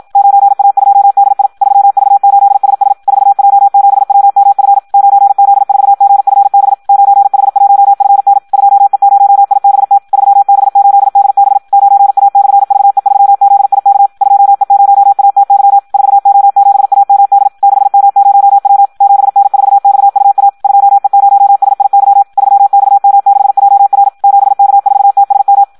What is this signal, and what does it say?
Morse code, heard as a keyed tone, about 53 wpm, 800 Hz, with radio noise. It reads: 9A2DI 3V8SS 4J7WMF 9Z4GFB 9H1FN 2E0IBT 3B8DB 9A/VE3ZIK 2E0ATY 4O7AMD 6M0W 7N4XUA 2E0SIY 3XM6JR 7K4TSM
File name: Audio practica CW / qrq-tx-54 2